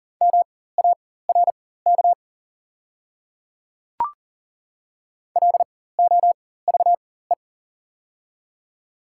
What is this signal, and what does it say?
Morse code, heard as a keyed tone, about 40 wpm, spaced as for 19 wpm, 700 Hz. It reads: MARK LOVE
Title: 4x Character Spacing - Top 500 Words - 40wpm